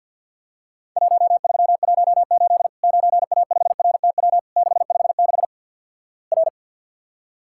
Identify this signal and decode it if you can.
Morse code, heard as a keyed tone, about 50 wpm, 700 Hz. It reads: 1318 9A5RTW 656 R
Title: Worked All Europe DX Contest - QTC Exchange - 50wpm